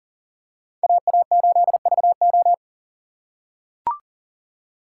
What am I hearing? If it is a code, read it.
Morse code, heard as a keyed tone, about 40 wpm, 700 Hz. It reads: AA8VO
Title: USA Call Signs - 5 Characters - 4x Character Spacing - 40wpm